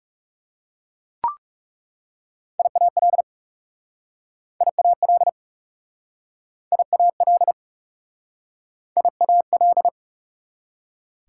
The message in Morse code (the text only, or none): IAL IAL IAL IAL